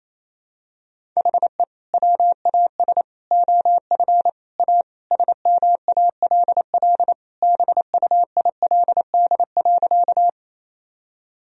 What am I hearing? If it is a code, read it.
Morse code, heard as a keyed tone, about 28 wpm, 700 Hz. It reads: HE WAS OF A SMALL BUILD.
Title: Sentences from Top 1000 Words - Review Part 1 - 28wpm